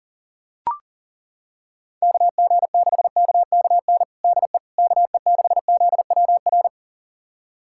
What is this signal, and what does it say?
Morse code, heard as a keyed tone, about 40 wpm, 700 Hz. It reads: KG6KKN DE KE6ZWR
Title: USA Callsign DE Callsign - 6 Characters - 40wpm